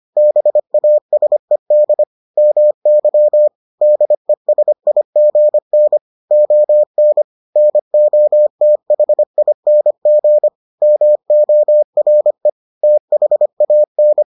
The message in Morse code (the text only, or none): BASED MY DESIGN ON NOTHING MORE THAN